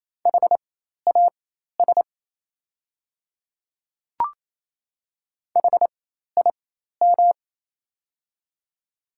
Morse code, heard as a keyed tone, about 28 wpm, spaced as for 13 wpm, 700 Hz. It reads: HAS HIM
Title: 4x Character Spacing - Top 100 Words - 28wpm